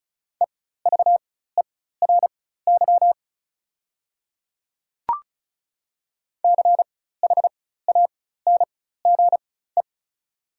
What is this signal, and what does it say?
Morse code, heard as a keyed tone, about 36 wpm, spaced as for 16 wpm, 700 Hz. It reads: EVERY CHANGE
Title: 4x Character Spacing - Top 200 Words - 35wpm